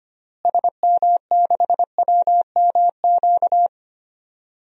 Morse code, heard as a keyed tone, about 25 wpm, 700 Hz. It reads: SM6WMQ
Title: DXCC-100 Call Signs - 6 Characters - Repeated 3x - 25wpm